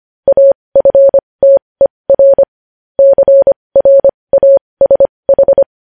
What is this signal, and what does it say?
Morse code, heard as a keyed tone, about 25 wpm, spaced as for 20 wpm, 550 Hz. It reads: AFTER CRASH
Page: news-headlines-in-morse-code-at-20-wpm